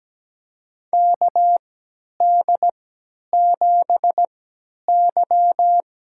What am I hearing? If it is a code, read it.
Morse code, heard as a keyed tone, about 17 wpm, spaced as for 10 wpm, 700 Hz. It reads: KD7Y
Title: USA Call Signs - 3x Character Spacing - 17wpm